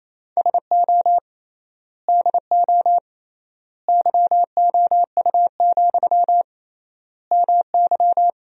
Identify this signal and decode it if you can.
Morse code, heard as a keyed tone, about 28 wpm, 700 Hz. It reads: SO DO YOU, MY
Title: 3x Word-Spacing - Sentences from Top 400 Words - 28wpm